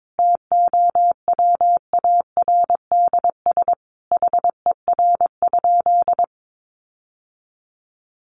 Morse code, heard as a keyed tone, about 22 wpm, 700 Hz. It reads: TOWARDS HER?
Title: Sentences from Top 1100 Words 22wpm